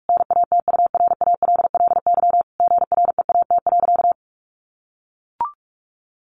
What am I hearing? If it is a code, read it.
Morse code, heard as a keyed tone, about 44 wpm, 700 Hz. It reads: NATURALLY GREAT.
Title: Sentences from Top 1300 Words 45wpm